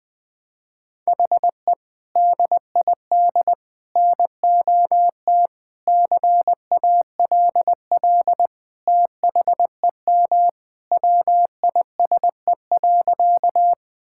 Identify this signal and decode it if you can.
Morse code, heard as a keyed tone, about 20 wpm, 700 Hz. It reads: HE DID NOT CALL THEM WISE.
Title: Sentences from Top 1600 Words 20wpm